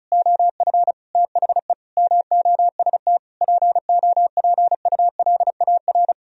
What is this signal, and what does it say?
Morse code, heard as a keyed tone, about 44 wpm, 700 Hz. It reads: OF THE MOST POPULAR
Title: Speed-Racing - Sets of 5 Words 35wpm